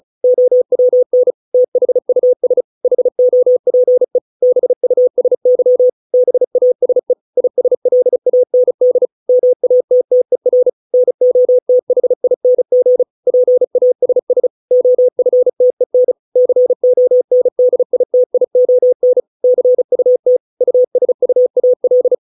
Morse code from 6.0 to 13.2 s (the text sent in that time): BASE ISLAND MATTER NOTHING